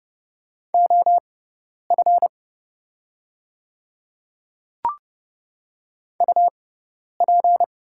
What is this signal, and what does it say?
Morse code, heard as a keyed tone, about 30 wpm, spaced as for 10 wpm, 700 Hz. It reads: OF UP
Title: 6x Character Spacing - Top 100 Words - 30wpm